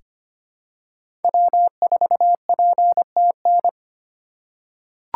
Morse code, heard as a keyed tone, about 25 wpm, 700 Hz. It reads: W4PTN
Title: USA Call Signs - 6x Character Spacing - 25wpm